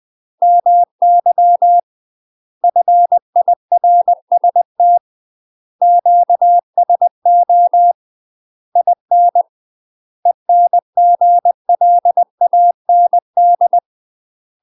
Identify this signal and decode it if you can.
Morse code, heard as a keyed tone, about 20 wpm, 700 Hz. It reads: MY FIRST QSO IN ENGLAND
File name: UK MASTER QSOs TRAINING/20WPM